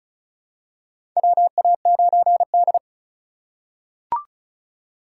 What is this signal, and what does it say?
Morse code, heard as a keyed tone, about 34 wpm, 700 Hz. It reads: WA9D
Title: USA Call Signs - 4 Characters - 3x Character Spacing - 35wpm